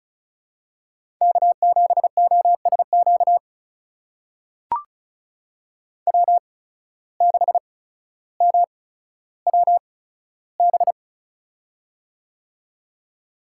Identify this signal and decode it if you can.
Morse code, heard as a keyed tone, about 34 wpm, spaced as for 15 wpm, 700 Hz. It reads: K7OSQ W6MWB
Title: USA Call Signs - 5 Characters - 8x Character Spacing - 35wpm